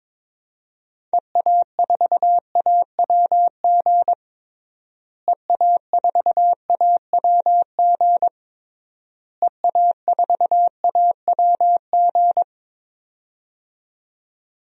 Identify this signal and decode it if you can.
Morse code, heard as a keyed tone, about 22 wpm, 700 Hz. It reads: EA4AWG EA4AWG EA4AWG
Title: DXCC-100 Call Signs - 6 Characters - Repeated 3x - 22wpm